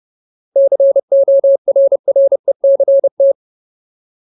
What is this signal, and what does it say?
Morse code, heard as a keyed tone, about 30 wpm, 550 Hz. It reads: CORRECT
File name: Ultimate Word List / 30WPM Ultimate